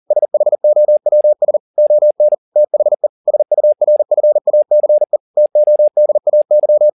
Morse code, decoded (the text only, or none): SHOWS ON THE SURFACE TODAY